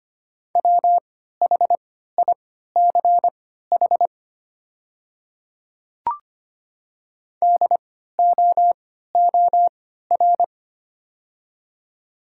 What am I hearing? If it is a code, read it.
Morse code, heard as a keyed tone, about 25 wpm, spaced as for 14 wpm, 690 Hz. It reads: WHICH DOOR
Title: 3x Character Spacing - Top 300 Words - 25wpm